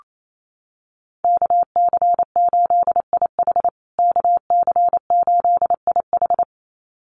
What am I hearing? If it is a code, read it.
Morse code, heard as a keyed tone, about 28 wpm, 700 Hz. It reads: KC8IH KC8IH
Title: USA Call Signs Repeated 2x 28wpm